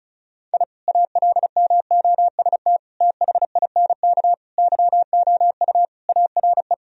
Morse code, heard as a keyed tone, about 36 wpm, 700 Hz. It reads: I ALMOST THINK YOU ARE
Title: Sentences from Top 1000 Words - Review Part 4 - 35wpm